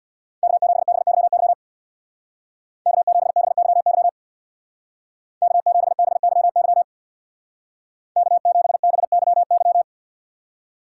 Morse code, heard as a keyed tone, about 64 wpm, 700 Hz. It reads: K7BYY K7BYY K7BYY K7BYY